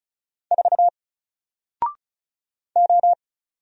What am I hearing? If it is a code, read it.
Morse code, heard as a keyed tone, about 34 wpm, 700 Hz. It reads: 4 O
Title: Single Letter-Number Alt 35wpm